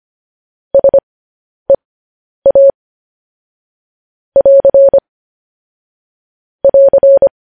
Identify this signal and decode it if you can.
Morse code, heard as a keyed tone, about 25 wpm, spaced as for 10 wpm, 550 Hz. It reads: SEA <AR> <AR>
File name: podcast / News headlines-1x-25x10